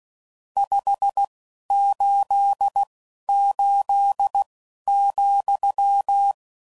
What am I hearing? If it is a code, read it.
Morse code, heard as a keyed tone, about 16 wpm, spaced as for 11 wpm, 800 Hz. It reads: 588,
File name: CW Code Groups @10WPM / QSOs / QSO WA7QOX